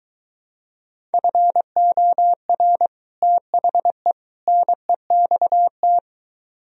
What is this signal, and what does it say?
Morse code, heard as a keyed tone, about 23 wpm, 700 Hz. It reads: FOR THE NEXT